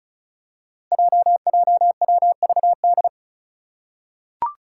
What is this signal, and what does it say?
Morse code, heard as a keyed tone, about 36 wpm, 700 Hz. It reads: JJWVD
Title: Code Groups - 5 Letters - 35wpm